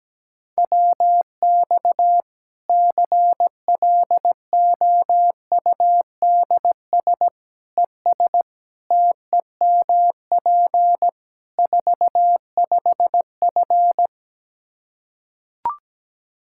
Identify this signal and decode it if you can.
Morse code, heard as a keyed tone, about 17 wpm, 700 Hz. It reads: WX CLOUDS ES TEMP 45F
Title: Practice QSO - Exchange 2 - Info - 17wpm